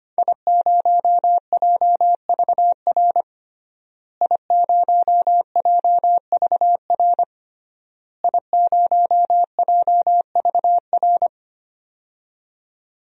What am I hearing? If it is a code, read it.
Morse code, heard as a keyed tone, about 25 wpm, 700 Hz. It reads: I0JVR I0JVR I0JVR
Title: DXCC-100 Call Signs - 5 Characters - Repeated 3x - 25wpm